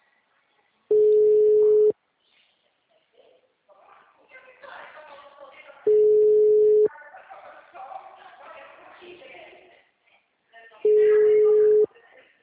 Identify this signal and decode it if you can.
Ringback tone (Europe, 425 Hz), with speech.